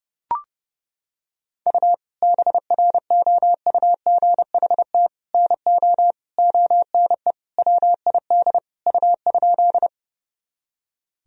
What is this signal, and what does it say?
Morse code, heard as a keyed tone, about 30 wpm, 700 Hz. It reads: U BROUGHT NO ONE WID U?
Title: Sentences from Top 500 Words and Abbreviations 30wpm